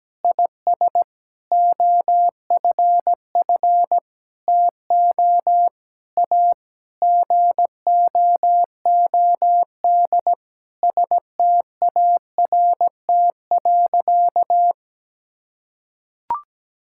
Morse code, heard as a keyed tone, about 17 wpm, 700 Hz. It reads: IS OFF TO A GOOD START.
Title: Sentences from Top 1600 Words 17wpm